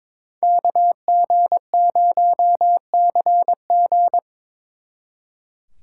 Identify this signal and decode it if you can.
Morse code, heard as a keyed tone, about 25 wpm, 700 Hz. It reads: KG0CG
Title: Speed Racing - USA Call Signs 22wpm